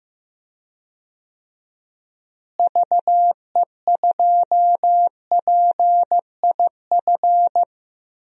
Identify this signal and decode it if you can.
Morse code, heard as a keyed tone, about 15 wpm, 700 Hz. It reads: VE2PIF
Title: DXCC-100 Call Signs - 6 Characters - Repeated 2x - 15wpm